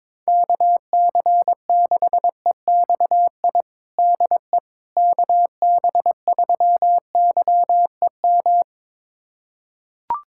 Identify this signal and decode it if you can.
Morse code, heard as a keyed tone, about 22 wpm, 700 Hz. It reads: KC6EXI DE KB3YEM
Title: USA Callsign DE Callsign 22wpm